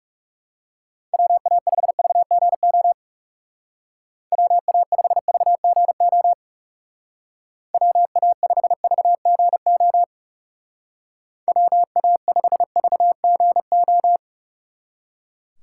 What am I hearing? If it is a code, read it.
Morse code, heard as a keyed tone, about 37 wpm, 700 Hz. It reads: WA5VGO WA5VGO WA5VGO WA5VGO
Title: Speed Racing - USA 2x3 Call Signs 30wpm